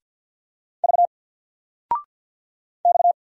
Morse code, heard as a keyed tone, about 50 wpm, 700 Hz. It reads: V X
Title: Single Letter-Number 50wpm